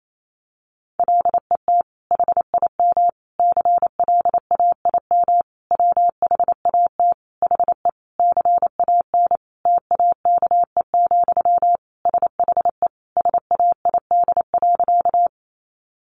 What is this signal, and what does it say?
Morse code, heard as a keyed tone, about 28 wpm, 700 Hz. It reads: LET HIM CLAIM WHAT HE CAN TAKE, SHE SAID.
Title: Sentences from Top 1500 Words 28wpm